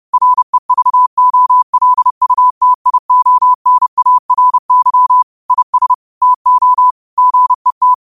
Morse code, heard as a keyed tone, about 30 wpm, 1000 Hz. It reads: REVOLUTIONARY IS TO GET